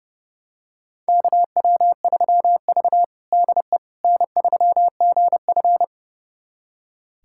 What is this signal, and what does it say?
Morse code, heard as a keyed tone, about 32 wpm, 700 Hz. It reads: KW3V DE N3GF
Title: USA Callsign DE Callsign - 4 Characters - 30wpm